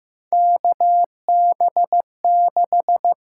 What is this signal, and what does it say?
Morse code, heard as a keyed tone, about 15 wpm, 700 Hz. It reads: KB6
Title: USA Call Signs - 6 Characters - 6x Character Spacing - 15wpm